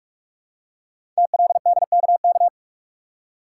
Morse code, heard as a keyed tone, about 44 wpm, 700 Hz. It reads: TLDKK
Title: Code Groups - 5 Letters - 45wpm